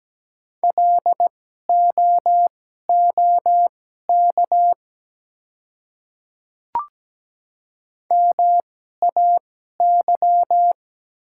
Morse code, heard as a keyed tone, about 17 wpm, spaced as for 12 wpm, 700 Hz. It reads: LOOK MAY
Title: 2x Character Spacing - Top 100 Words - 17wpm